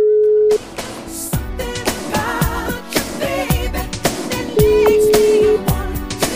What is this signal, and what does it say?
Ringback tone (Europe, 425 Hz), with noise.